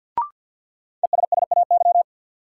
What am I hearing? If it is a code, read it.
Morse code, heard as a keyed tone, about 50 wpm, 700 Hz. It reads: ESSAY
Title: Sentences from Top 1500 Words - New Word Drill Alt - 50wpm